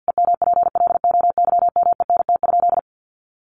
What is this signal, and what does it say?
Morse code, heard as a keyed tone, about 50 wpm, 700 Hz. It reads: EMPLOYMENT?
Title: Sentences from Top 1500 Words 50wpm